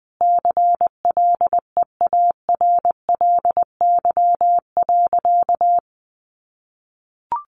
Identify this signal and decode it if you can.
Morse code, heard as a keyed tone, about 20 wpm, 700 Hz. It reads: CLEARLY.
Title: Sentences from Top 1300 Words 20wpm